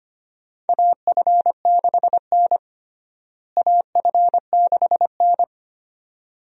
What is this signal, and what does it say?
Morse code, heard as a keyed tone, about 25 wpm, 700 Hz. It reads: AF6N AF6N
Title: USA Call Signs - 4 Characters - Repeated 2x - 25wpm